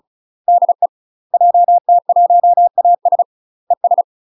Morse code, heard as a keyed tone, about 34 wpm, 700 Hz. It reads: DE JT1AS ES
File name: UK MASTER QSOs TRAINING / 35WPM / UK MASTER QSOs 0019